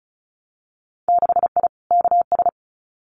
Morse code, heard as a keyed tone, about 36 wpm, 700 Hz. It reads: 6I KS
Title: Winter Field Day Exchanges Encore 35wpm